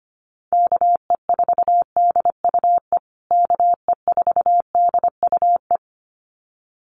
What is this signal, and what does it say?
Morse code, heard as a keyed tone, about 25 wpm, 700 Hz. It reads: KE4DUE KE4DUE
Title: USA Call Signs Repeated 2x 25wpm